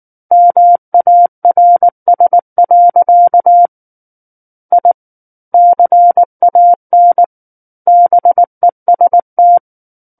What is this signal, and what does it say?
Morse code, heard as a keyed tone, about 19 wpm, 700 Hz. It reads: MARS. I CAN BEST
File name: Book One / 10-Chapter 10 - 19 WPM